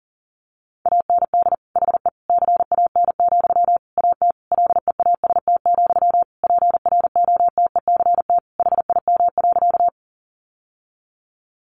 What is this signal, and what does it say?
Morse code, heard as a keyed tone, about 40 wpm, 700 Hz. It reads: AND HE CAN, AT LEAST, PROTECT HIM.
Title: Sentences from Top 900 Words 40wpm